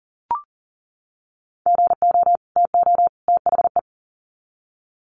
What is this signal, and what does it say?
Morse code, heard as a keyed tone, about 40 wpm, 700 Hz. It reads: GO TO THE
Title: Sets of 3 Words 40wpm